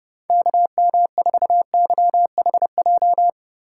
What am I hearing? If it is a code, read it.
Morse code, heard as a keyed tone, about 30 wpm, 700 Hz. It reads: KM4YHJ
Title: USA Call Signs - 2x Character Spacing - 30wpm